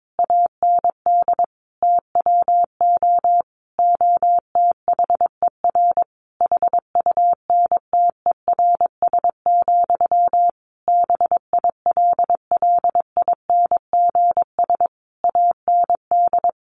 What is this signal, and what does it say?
Morse code, heard as a keyed tone, about 22 wpm, 700 Hz. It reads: AND TWO OTHER HUNTERS, BILLINGS AND
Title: A Sound Of Thunder 22wpm